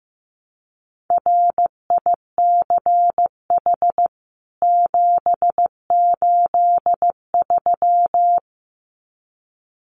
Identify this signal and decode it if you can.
Morse code, heard as a keyed tone, about 15 wpm, 700 Hz. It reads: RICH 783
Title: CWT Exchange 15wpm